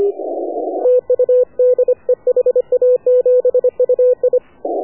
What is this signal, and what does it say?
Morse code, heard as a keyed tone, about 24 wpm, 500 Hz. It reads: TU DE HA7UI
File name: AUDIO2019 / HA7UI CW